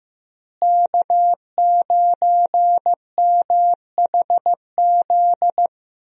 Morse code, heard as a keyed tone, about 15 wpm, 700 Hz. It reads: K9MHZ
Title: USA Call Signs - 5 Characters - Repeated 2x - 15wpm